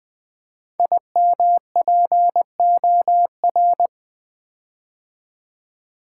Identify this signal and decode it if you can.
Morse code, heard as a keyed tone, about 20 wpm, 700 Hz. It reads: IMPOR
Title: Build A Word - 5 Letters At A Time - 10-Letter Words - 20wpm